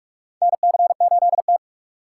Morse code, heard as a keyed tone, about 46 wpm, 700 Hz. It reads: NC8T
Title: USA Call Signs - 4 Characters - 3x Character Spacing - 45wpm